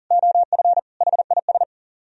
Morse code, heard as a keyed tone, about 52 wpm, 700 Hz. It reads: OF HIS